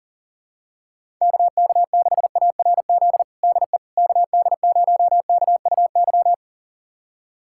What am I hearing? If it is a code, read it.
Morse code, heard as a keyed tone, about 37 wpm, 700 Hz. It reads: KK6ARZ DE KD0KUY
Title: USA Callsign DE Callsign 40wpm